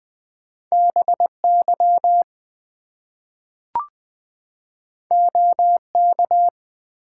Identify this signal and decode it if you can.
Morse code, heard as a keyed tone, about 20 wpm, 700 Hz. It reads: BY OK
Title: Two Letter Words - 20wpm